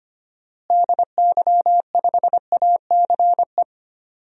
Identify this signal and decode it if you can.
Morse code, heard as a keyed tone, about 25 wpm, 700 Hz. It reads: DY5ACE
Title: DXCC-100 Call Signs - 6 Characters - Repeated 2x - 25wpm